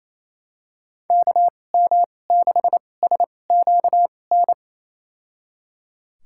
Morse code, heard as a keyed tone, about 28 wpm, spaced as for 20 wpm, 700 Hz. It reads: KM6SQN
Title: USA Call Signs - 6 Characters - 2x Character Spacing - 28wpm